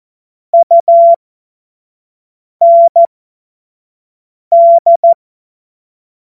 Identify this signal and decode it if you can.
Morse code, heard as a keyed tone, about 14 wpm, spaced as for 5 wpm, 680 Hz. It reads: UND
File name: Morsekurs / Witze / Witz 29 April 2025 25 F70